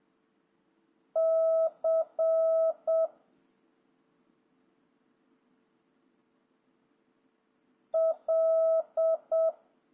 Morse code, heard as a keyed tone, about 7 wpm, 650 Hz. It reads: C L
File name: jh8cbh / 2026cw